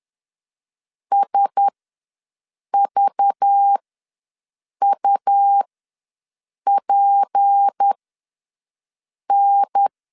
Morse code, heard as a keyed tone, about 10 wpm, spaced as for 6 wpm, 800 Hz. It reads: SVUPN
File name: LEZIONI / 7 GF9Z / 7 GF9Z ws 10wpm 3919